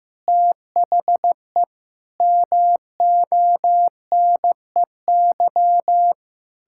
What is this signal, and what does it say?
Morse code, heard as a keyed tone, about 15 wpm, 700 Hz. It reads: THE MONEY
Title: Pangram Challenge v2 - 15wpm